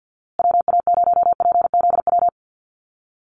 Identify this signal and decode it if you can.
Morse code, heard as a keyed tone, about 48 wpm, 700 Hz. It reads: WA0PZW
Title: USA Call Signs Repeated 3x 50wpm